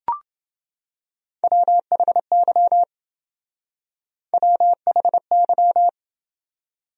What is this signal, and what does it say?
Morse code, heard as a keyed tone, about 28 wpm, 700 Hz. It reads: WHY WHY